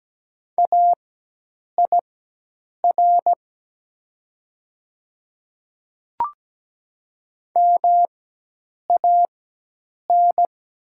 Morse code, heard as a keyed tone, about 17 wpm, spaced as for 8 wpm, 700 Hz. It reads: AIR MAN
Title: 4x Character Spacing - Top 200 Words - 17wpm